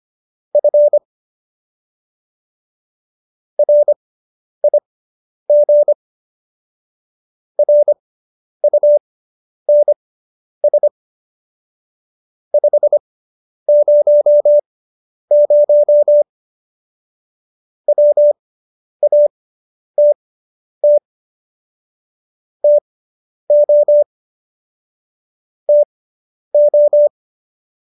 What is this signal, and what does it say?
Morse code, heard as a keyed tone, about 25 wpm, spaced as for 10 wpm, 590 Hz. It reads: F RIG RUNS 500 WATT TO TO